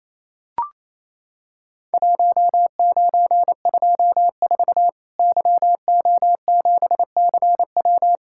Morse code, heard as a keyed tone, about 28 wpm, 700 Hz. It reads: 1924 YO7CW
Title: Worked All Europe DX Contest - QTC Exchange - 28wpm